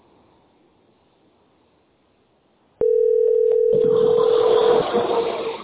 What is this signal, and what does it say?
Ringback tone (North America), 440 + 480 Hz.